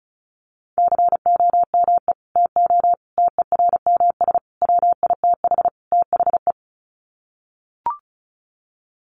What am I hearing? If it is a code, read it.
Morse code, heard as a keyed tone, about 34 wpm, 700 Hz. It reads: COME TO TERMS WITH THE
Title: Sets of 5 Words Encore2 35wpm